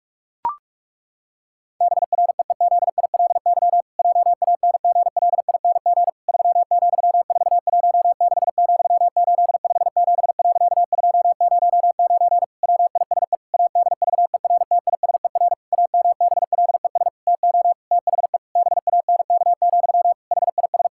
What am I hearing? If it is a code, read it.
Morse code, heard as a keyed tone, about 46 wpm, 700 Hz. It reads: BREEZILY JANGLING 3,416,857.209 WISE ADVERTISER AMBLES TO THE BANK, HIS
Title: Pangram Challenge v2 - 45wpm